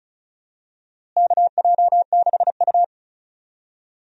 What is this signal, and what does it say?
Morse code, heard as a keyed tone, about 36 wpm, 700 Hz. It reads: KJ6U
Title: USA Call Signs - 4 Characters - Repeated 3x - 35wpm